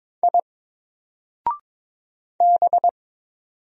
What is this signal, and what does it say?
Morse code, heard as a keyed tone, about 22 wpm, 700 Hz. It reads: I B
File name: Single-Letter-Number-22wpm